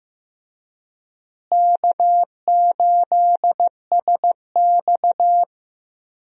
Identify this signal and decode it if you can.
Morse code, heard as a keyed tone, about 15 wpm, 700 Hz. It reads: K8SX
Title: USA Call Signs - 4 Characters - Repeated 3x - 15wpm